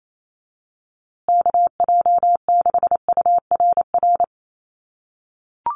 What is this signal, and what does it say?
Morse code, heard as a keyed tone, about 28 wpm, 700 Hz. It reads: KJ6URR
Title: USA Call Signs Repeated 2x 28wpm